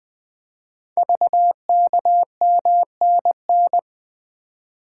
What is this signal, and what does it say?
Morse code, heard as a keyed tone, about 20 wpm, 700 Hz. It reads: VKMNN
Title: Code Groups - 5 Letters - 20wpm